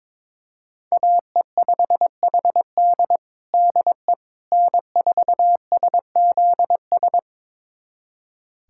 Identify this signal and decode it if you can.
Morse code, heard as a keyed tone, about 22 wpm, 700 Hz. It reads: AE5HD DE N4SZS
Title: USA Callsign DE Callsign - 5 Characters - 22wpm